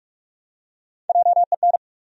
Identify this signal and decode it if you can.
Morse code, heard as a keyed tone, about 47 wpm, 700 Hz. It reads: JEN
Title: USA Common Names - Max 4 Letters - 45wpm